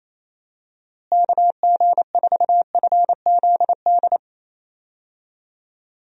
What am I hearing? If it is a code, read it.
Morse code, heard as a keyed tone, about 27 wpm, 700 Hz. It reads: KG4FZD
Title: USA Call Signs - 6 Characters - Repeated 2x - 28wpm